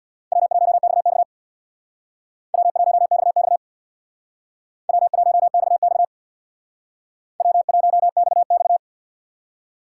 Morse code, heard as a keyed tone, about 64 wpm, 700 Hz. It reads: W1XX W1XX W1XX W1XX